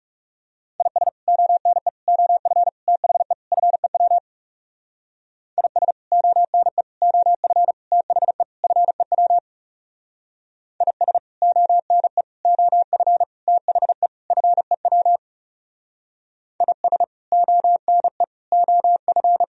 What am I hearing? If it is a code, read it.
Morse code, heard as a keyed tone, about 36 wpm, 700 Hz. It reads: IS ONE OF THE FEW IS ONE OF THE FEW IS ONE OF THE FEW IS ONE OF
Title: Speed-Racing - Sets of 5 Words 30wpm